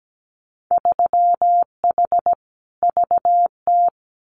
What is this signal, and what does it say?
Morse code, heard as a keyed tone, about 17 wpm, 700 Hz. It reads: 3H VT
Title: Winter Field Day Exchanges Encore 17wpm